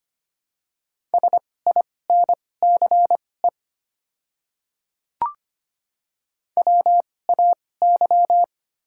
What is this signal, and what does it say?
Morse code, heard as a keyed tone, about 25 wpm, spaced as for 18 wpm, 700 Hz. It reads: SINCE WAY
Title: 2x Character Spacing - Top 300 Words - 25wpm